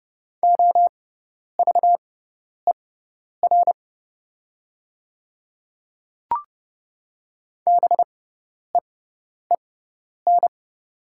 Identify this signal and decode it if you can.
Morse code, heard as a keyed tone, about 29 wpm, spaced as for 10 wpm, 700 Hz. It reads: OVER BEEN
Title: 6x Character Spacing - Top 100 Words - 30wpm